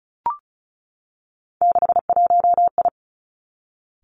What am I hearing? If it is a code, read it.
Morse code, heard as a keyed tone, about 36 wpm, 700 Hz. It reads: 61I
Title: Three Challenging Letter-Number Combinations 35wpm